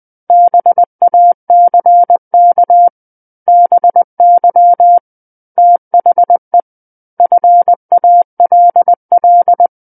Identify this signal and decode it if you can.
Morse code, heard as a keyed tone, about 20 wpm, 700 Hz. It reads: BACK BY THE FALL
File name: 20 WPM